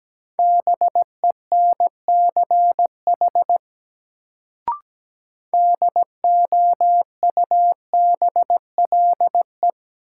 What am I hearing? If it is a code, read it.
Morse code, heard as a keyed tone, about 17 wpm, 700 Hz. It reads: BENCH DOUBLE
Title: Sentences from Top 1400 Words - New Word Drill - 17wpm